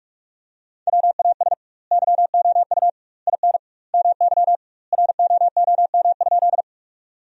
Morse code, heard as a keyed tone, about 41 wpm, 700 Hz. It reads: WAS YOU IN MY ROOM?
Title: 2x Word-Spacing - Sentences from Top 300 Words - 45wpm